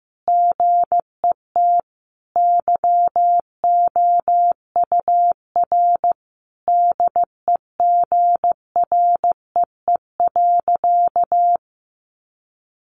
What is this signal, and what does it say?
Morse code, heard as a keyed tone, about 15 wpm, 700 Hz. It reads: GET YOUR DEGREE.
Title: Sentences from Top 1500 Words 15wpm